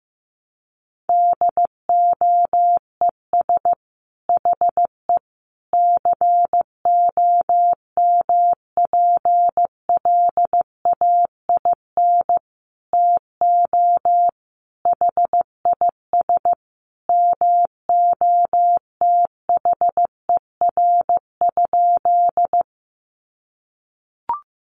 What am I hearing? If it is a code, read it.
Morse code, heard as a keyed tone, about 15 wpm, 700 Hz. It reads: DOES HE COMPLAIN TO HIS MOTHER?
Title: Sentences from Top 1300 Words 15wpm